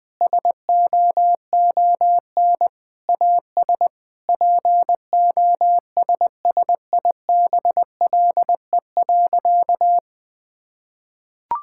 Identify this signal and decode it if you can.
Morse code, heard as a keyed tone, about 20 wpm, 700 Hz. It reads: SOON AS POSSIBLE.
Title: Sentences from Top 1000 Words - Review Part 5 - 20wpm